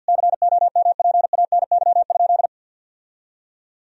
Morse code, heard as a keyed tone, about 52 wpm, 700 Hz. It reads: COMPANY?
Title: Sentences from Top 1000 Words - Review Part 3 - 50wpm